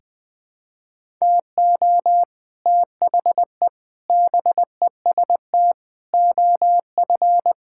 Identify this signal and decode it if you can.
Morse code, heard as a keyed tone, about 20 wpm, 700 Hz. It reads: TO THE BEST OF
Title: Sentences with 6-Word Phrases - 20wpm